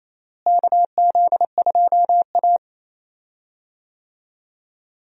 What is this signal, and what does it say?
Morse code, heard as a keyed tone, about 28 wpm, 700 Hz. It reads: KZ2A
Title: USA Call Signs - 4 Characters - Repeated 2x - 28wpm